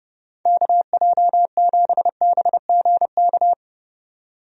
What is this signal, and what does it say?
Morse code, heard as a keyed tone, about 29 wpm, 700 Hz. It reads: KJ7BGK
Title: USA Call Signs - 6 Characters - 30wpm